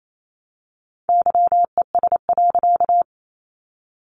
Morse code, heard as a keyed tone, about 28 wpm, 700 Hz. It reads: YES.
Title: A Sound Of Thunder 28wpm